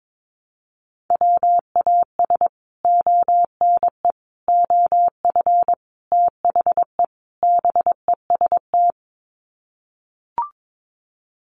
Morse code, heard as a keyed tone, about 22 wpm, 700 Hz. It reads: WAS ONE OF THE BEST
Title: Sets of 5 Words Encore2 22wpm